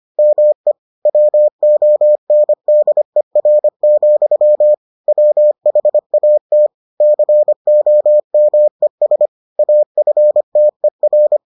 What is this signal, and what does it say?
Morse code, heard as a keyed tone, about 25 wpm, 600 Hz. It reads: ME WONDER, WHAT COMES AFTER